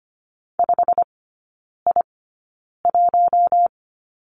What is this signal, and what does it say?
Morse code, heard as a keyed tone, about 25 wpm, spaced as for 9 wpm, 700 Hz. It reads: 5I1
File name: Lesson 15e - Number 5 - Three Characters 25wpm